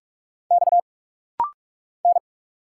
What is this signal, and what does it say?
Morse code, heard as a keyed tone, about 46 wpm, 700 Hz. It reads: X N
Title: Single Letters - Rapid-Fire 45wpm